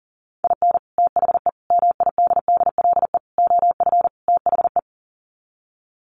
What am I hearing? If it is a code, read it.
Morse code, heard as a keyed tone, about 40 wpm, 700 Hz. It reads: IN THE MIDDLE OF THE
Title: Sets of 5 Words 40wpm